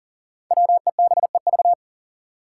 Morse code, heard as a keyed tone, about 39 wpm, 700 Hz. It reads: WEBEV
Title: Five Challenging Letter-Number Combinations 40wpm